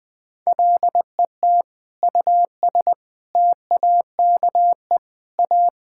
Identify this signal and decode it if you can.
Morse code, heard as a keyed tone, about 19 wpm, 700 Hz. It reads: LET US TAKE A
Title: Sentences with 6-Word Phrases - Spoken Beginning - 20wpm